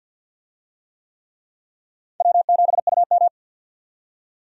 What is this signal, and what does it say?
Morse code, heard as a keyed tone, about 51 wpm, 700 Hz. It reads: W7UM
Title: USA Call Signs - 4 Characters - 50wpm